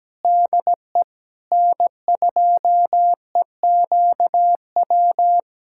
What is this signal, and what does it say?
Morse code, heard as a keyed tone, about 17 wpm, 700 Hz. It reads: DE N2EQW
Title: USA Callsign DE Callsign - 5 Characters - 17wpm